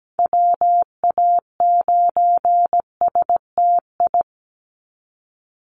Morse code, heard as a keyed tone, about 17 wpm, 700 Hz. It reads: WA9STI
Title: NA SOTA Activator Callsigns 17wpm